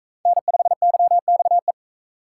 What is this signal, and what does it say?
Morse code, heard as a keyed tone, about 44 wpm, 700 Hz. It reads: N5YXE